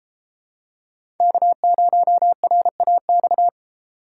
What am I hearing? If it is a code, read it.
Morse code, heard as a keyed tone, about 33 wpm, 700 Hz. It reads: K0RAX